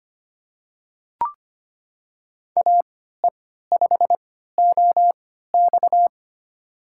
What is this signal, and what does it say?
Morse code, heard as a keyed tone, about 25 wpm, spaced as for 14 wpm, 700 Hz. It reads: AE5OX
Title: USA Call Signs - 3x Character Spacing - 25wpm